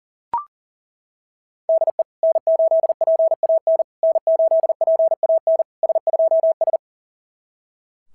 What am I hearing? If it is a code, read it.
Morse code, heard as a keyed tone, about 40 wpm, 690 Hz. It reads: DE N8PAN N8PAN S2S
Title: SOTA QSO Practice 40wpm